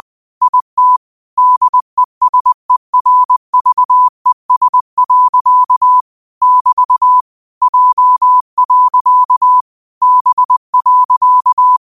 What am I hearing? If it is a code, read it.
Morse code, heard as a keyed tone, about 20 wpm, 1000 Hz. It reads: IT DESERVES. <BT> J. B.
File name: qotdfiles / 20wpm